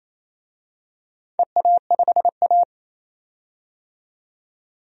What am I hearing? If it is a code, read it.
Morse code, heard as a keyed tone, about 29 wpm, 700 Hz. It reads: EA5A
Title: DXCC-100 Call Signs - 4 Characters - 28wpm